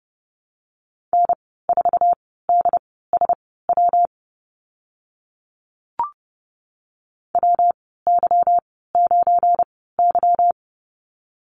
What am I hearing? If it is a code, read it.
Morse code, heard as a keyed tone, about 30 wpm, spaced as for 17 wpm, 700 Hz. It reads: N4DSW WY9Y
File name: Lesson 37g - Letter x - Call Signs 30wpm